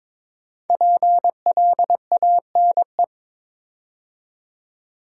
Morse code, heard as a keyed tone, about 22 wpm, 700 Hz. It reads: PLANE
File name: 5 Letter Words - First 3 Letters Given - 22wpm